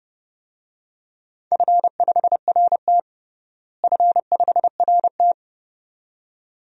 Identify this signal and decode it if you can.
Morse code, heard as a keyed tone, about 32 wpm, 700 Hz. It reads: F5RT F5RT
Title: DXCC-100 Call Signs - 4 Characters - Repeated 2x - 30wpm